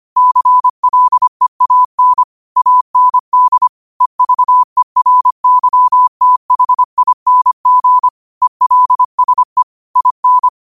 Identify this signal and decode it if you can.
Morse code, heard as a keyed tone, about 25 wpm, 1000 Hz. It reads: CLEAN AND EVERYTHING ELSE IN
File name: qotdfiles / 25wpm